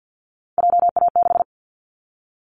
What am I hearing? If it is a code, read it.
Morse code, heard as a keyed tone, about 49 wpm, 700 Hz. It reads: JA6
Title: Callsign Prefixes for Top 25 DXCC - Repeated 3x - 50wpm